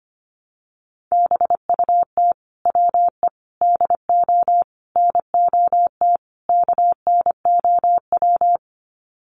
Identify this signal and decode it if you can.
Morse code, heard as a keyed tone, about 25 wpm, 700 Hz. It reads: BUT WE DO NOT KNOW
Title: Sets of 5 Words Encore2 25wpm